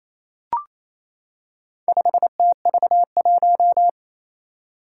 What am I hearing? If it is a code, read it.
Morse code, heard as a keyed tone, about 27 wpm, 700 Hz. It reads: 5TV1
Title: Four Challenging Letter-Number Combinations 28wpm